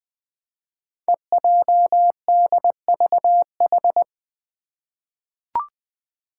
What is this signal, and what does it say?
Morse code, heard as a keyed tone, about 20 wpm, 700 Hz. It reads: EJDVH
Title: Five Challenging Letter-Number Combinations 20wpm